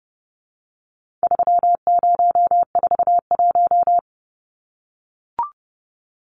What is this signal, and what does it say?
Morse code, heard as a keyed tone, about 30 wpm, 700 Hz. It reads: 3041
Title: Numbers - 4-Digit 30wpm